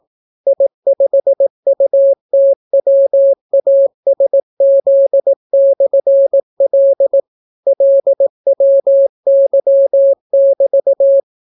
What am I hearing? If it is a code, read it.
Morse code, heard as a keyed tone, about 18 wpm, 550 Hz. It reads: I5UTWASZ/L LWY<BT>